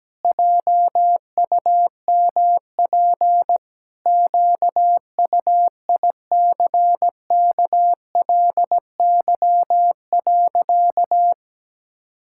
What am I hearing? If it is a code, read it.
Morse code, heard as a keyed tone, about 17 wpm, 700 Hz. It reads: JUMP QUICKLY.
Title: Pangram Challenge v2 - 17wpm